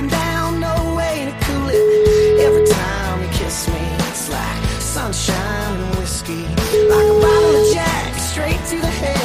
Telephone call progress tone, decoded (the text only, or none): ringback tone